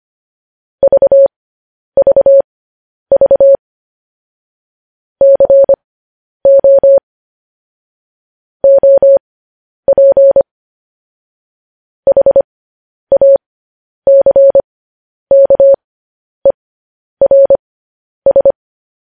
Morse code, heard as a keyed tone, about 25 wpm, spaced as for 10 wpm, 550 Hz. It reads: VVV CO OP HACKERS
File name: podcast / News headlines-1x-25x10